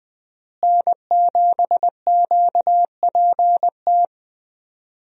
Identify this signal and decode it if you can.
Morse code, heard as a keyed tone, about 21 wpm, 700 Hz. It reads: N7QPT
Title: USA Call Signs - 5 Characters - Repeated 3x - 20wpm